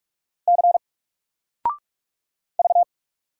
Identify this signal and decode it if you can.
Morse code, heard as a keyed tone, about 46 wpm, 700 Hz. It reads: C V
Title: Single Character (Letter:Number:Punctuation:Prosign) 45wpm